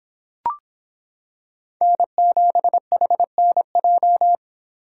Morse code, heard as a keyed tone, about 26 wpm, 700 Hz. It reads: N7HNJ